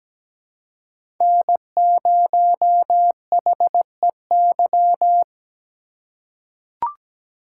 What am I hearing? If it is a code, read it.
Morse code, heard as a keyed tone, about 17 wpm, 700 Hz. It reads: N0HEY
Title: USA Call Signs - 5 Characters - 2x Character Spacing - 17wpm